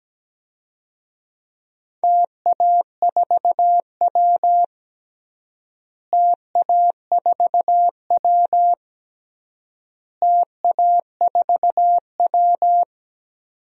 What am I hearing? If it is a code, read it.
Morse code, heard as a keyed tone, about 17 wpm, 700 Hz. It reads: TA4W TA4W TA4W
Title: DXCC-100 Call Signs - 4 Characters - Repeated 3x - 17wpm